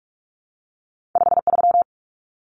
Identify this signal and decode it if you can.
Morse code, heard as a keyed tone, about 46 wpm, 700 Hz. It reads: 53
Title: Numbers - 2-Digit 45wpm